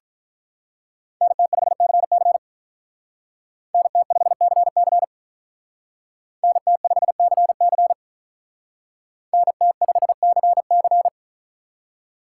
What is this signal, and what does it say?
Morse code, heard as a keyed tone, about 42 wpm, 700 Hz. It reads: NT5CC NT5CC NT5CC NT5CC
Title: Speed Racing - USA 2x2 Call Signs 35wpm